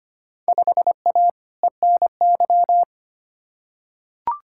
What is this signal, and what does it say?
Morse code, heard as a keyed tone, about 25 wpm, 700 Hz. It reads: 5A ENY
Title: Field Day Exchanges - 25wpm